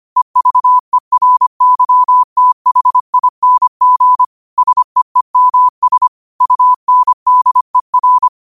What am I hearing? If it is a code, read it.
Morse code, heard as a keyed tone, about 25 wpm, 1000 Hz. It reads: EVERYTHING SEEMS UNDER
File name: qotdfiles / 25wpm